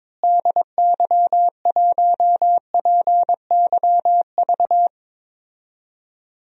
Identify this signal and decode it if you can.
Morse code, heard as a keyed tone, about 22 wpm, 700 Hz. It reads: DY1PYV
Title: DXCC-100 Call Signs - 6 Characters - Repeated 3x - 22wpm